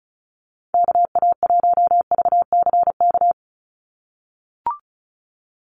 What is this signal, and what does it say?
Morse code, heard as a keyed tone, about 34 wpm, 700 Hz. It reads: KA1VCK
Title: USA Call Signs 35wpm